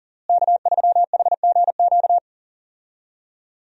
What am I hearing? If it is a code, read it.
Morse code, heard as a keyed tone, about 40 wpm, 700 Hz. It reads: K3HGQ
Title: USA Call Signs - 5 Characters - Repeated 3x - 40wpm